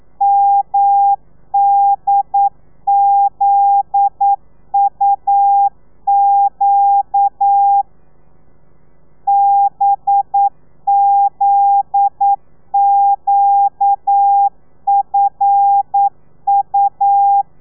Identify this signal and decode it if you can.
Morse code, heard as a keyed tone, about 9 wpm, 800 Hz. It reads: MDZUQ BZQFU
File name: telegrafia / i2tfj / Lez9 40